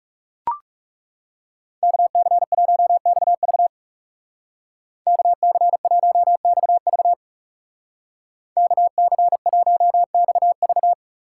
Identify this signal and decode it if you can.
Morse code, heard as a keyed tone, about 38 wpm, 700 Hz. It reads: KC1XV KC1XV KC1XV